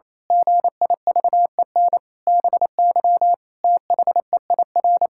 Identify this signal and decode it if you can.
Morse code, heard as a keyed tone, about 27 wpm, 700 Hz. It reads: GIVEN BY THEIR
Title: Pangram Challenge v2 - 28wpm